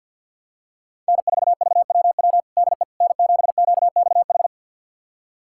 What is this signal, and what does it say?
Morse code, heard as a keyed tone, about 51 wpm, 690 Hz. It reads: N4VWW DE N7QXH
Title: USA Callsign DE Callsign - 5 Characters - 50wpm